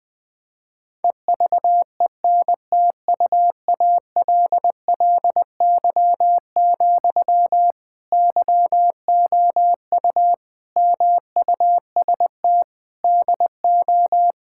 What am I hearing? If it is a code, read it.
Morse code, heard as a keyed tone, about 21 wpm, 700 Hz. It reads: EVENTUALLY, YOU MUST DO
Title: Sentences from Top 1000 Words - Review Part 6 - 20wpm